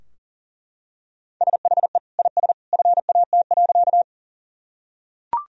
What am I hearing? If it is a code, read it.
Morse code, heard as a keyed tone, about 37 wpm, 700 Hz. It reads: SHE IS FAT.
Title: Sentences from Top 1000 Words - Review Part 4 - 40wpm